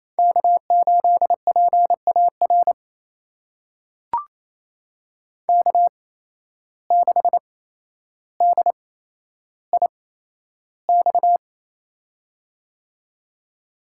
Morse code, heard as a keyed tone, about 28 wpm, spaced as for 12 wpm, 700 Hz. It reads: K8PAR K6DIX
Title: USA Call Signs - 5 Characters - 8x Character Spacing - 28wpm